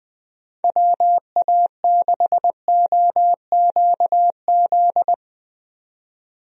Morse code, heard as a keyed tone, about 20 wpm, 700 Hz. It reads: WA6OQZ
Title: USA Call Signs - 6 Characters - 20wpm